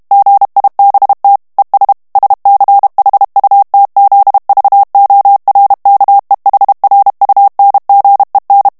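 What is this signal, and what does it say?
Morse code, heard as a keyed tone, about 32 wpm, 800 Hz. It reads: GIBT ES SCHUTZVORKEHRUNGEN
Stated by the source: German plain-text Morse reading practice at 30 words per minute.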